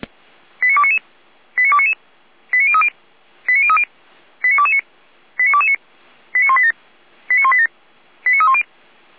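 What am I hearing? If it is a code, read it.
Five-tone selective call (ZVEI), heard as quick runs of five tones, tones of about 70 ms. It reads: 89299 89299 89030 89030 89229 89229 89187 89187 89310
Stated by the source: ZVEI five-tone selective call 89240 received by SDR on channel 455, from feuerwehr-orlamuende.org fire brigade audio archive.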